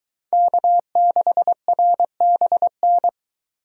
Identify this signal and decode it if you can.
Morse code, heard as a keyed tone, about 22 wpm, 700 Hz. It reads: K6RBN